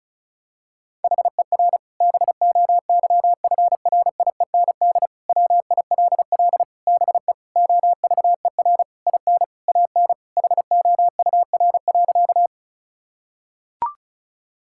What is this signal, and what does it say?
Morse code, heard as a keyed tone, about 34 wpm, 700 Hz. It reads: HER BOYFRIEND WILL BE OVER IN AN HOUR.
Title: Sentences from Top 1000 Words - Review Part 5 - 35wpm